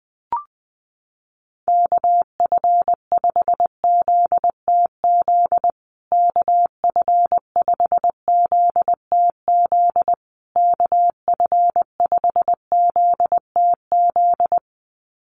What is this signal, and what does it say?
Morse code, heard as a keyed tone, about 20 wpm, 700 Hz. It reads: KF5ZTZ KF5ZTZ KF5ZTZ
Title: USA Call Signs Repeated 3x 20wpm